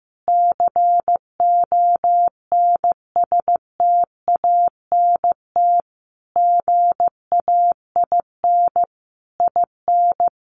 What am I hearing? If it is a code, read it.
Morse code, heard as a keyed tone, about 15 wpm, 700 Hz. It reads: CONSTANT GAIN IN
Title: Sentences from Top 1500 Words 15wpm